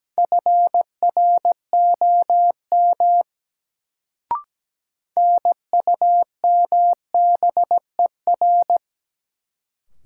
Morse code, heard as a keyed tone, about 17 wpm, 700 Hz. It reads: FROM NUMBER
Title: Sentences from Top 100 Words - New Word Drill - 17wpm